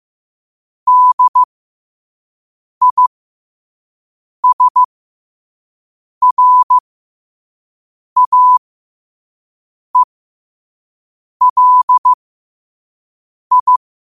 Morse code, heard as a keyed tone, about 15 wpm, spaced as for 5 wpm, 1000 Hz. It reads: DISRAELI
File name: qotdfiles / 5wpm